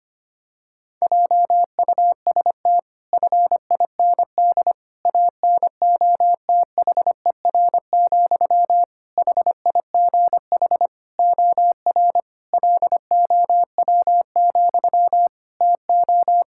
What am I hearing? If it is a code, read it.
Morse code, heard as a keyed tone, about 25 wpm, 700 Hz. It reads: JUST FIND ANOTHER, HIGH OR LOW, TO
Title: Sentences from Top 1600 Words 25wpm